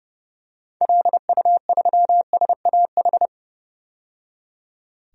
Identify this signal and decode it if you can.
Morse code, heard as a keyed tone, about 29 wpm, 700 Hz. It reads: LU3SAH
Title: DXCC-100 Call Signs - 6 Characters - 30wpm